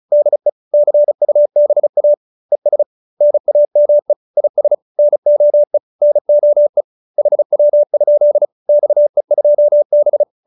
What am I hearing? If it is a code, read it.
Morse code, heard as a keyed tone, about 34 wpm, 590 Hz. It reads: DE CUBA ES NAME IS NOE NOE HW? XE2B